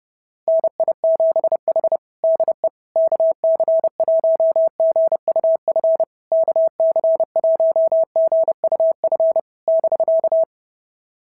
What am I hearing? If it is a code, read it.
Morse code, heard as a keyed tone, about 29 wpm, 690 Hz. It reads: NI7H DE KC1GUF KC1GUF <BK>
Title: SOTA QSO Practice 30wpm